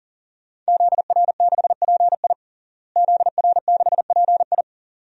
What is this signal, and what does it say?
Morse code, heard as a keyed tone, about 40 wpm, 700 Hz. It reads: ZR6PI ZR6PI
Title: DXCC-100 Call Signs - 5 Characters - Repeated 2x - 40wpm